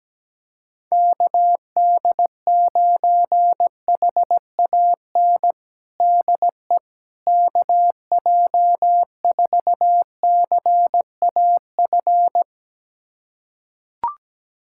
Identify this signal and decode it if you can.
Morse code, heard as a keyed tone, about 17 wpm, 700 Hz. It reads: KD9HAN DE KJ4CAF
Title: USA Callsign DE Callsign 17wpm